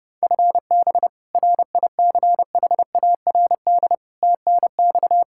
Morse code, heard as a keyed tone, about 32 wpm, 700 Hz. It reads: FB RICHARD TNX
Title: Practice QSO - Exchange 3 - More Info - 30wpm